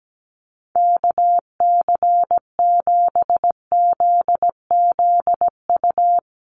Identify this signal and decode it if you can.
Morse code, heard as a keyed tone, about 17 wpm, 700 Hz. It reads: KC7ZZU
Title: USA Call Signs 17wpm